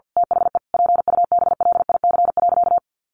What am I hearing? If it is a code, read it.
Morse code, heard as a keyed tone, about 50 wpm, 700 Hz. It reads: THE PUBLIC.
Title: Sentences from Top 1500 Words 50wpm